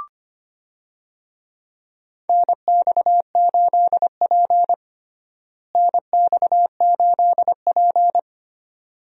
Morse code, heard as a keyed tone, about 25 wpm, 700 Hz. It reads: NX8P NX8P
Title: USA Call Signs - 4 Characters - Repeated 2x - 25wpm